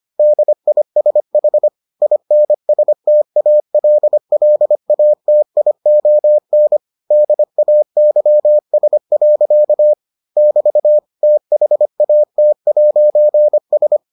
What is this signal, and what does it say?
Morse code, heard as a keyed tone, about 25 wpm, 600 Hz. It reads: DISH INSTALLATION DAYS. <BT> THAT'S